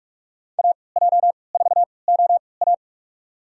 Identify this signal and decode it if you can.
Morse code, heard as a keyed tone, about 46 wpm, spaced as for 25 wpm, 700 Hz. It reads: AJ4OA
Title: USA Call Signs - 3x Character Spacing - 45wpm